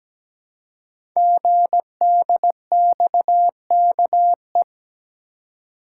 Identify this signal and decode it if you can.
Morse code, heard as a keyed tone, about 17 wpm, 700 Hz. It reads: GDXKE
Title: Code Groups - 5 Letters - 17wpm